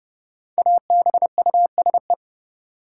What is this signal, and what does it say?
Morse code, heard as a keyed tone, about 31 wpm, 700 Hz. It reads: ABUSE
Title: Sentences from Top 1300 Words - New Word Drill Alt - 30wpm